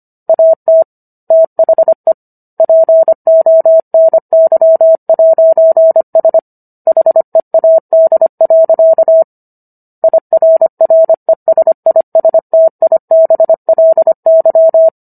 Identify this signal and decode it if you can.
Morse code, heard as a keyed tone, about 25 wpm, 650 Hz. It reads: AT THE PONY'S HEAD. IRRESISTIBLY